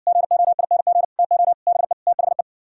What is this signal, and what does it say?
Morse code, heard as a keyed tone, about 60 wpm, 700 Hz. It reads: GOING TO BE THE